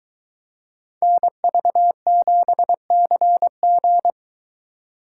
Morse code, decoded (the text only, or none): NV7CG